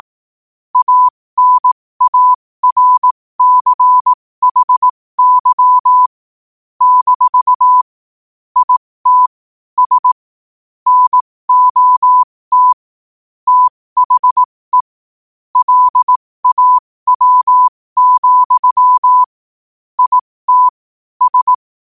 Morse code, heard as a keyed tone, about 18 wpm, spaced as for 15 wpm, 1000 Hz. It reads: ANARCHY - IT S NOT THE LAW, IT S
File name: qotdfiles / 15wpm